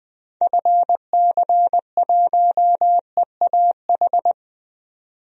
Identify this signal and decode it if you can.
Morse code, heard as a keyed tone, about 20 wpm, 700 Hz. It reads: FC1EAH
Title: DXCC-100 Call Signs - 6 Characters - 20wpm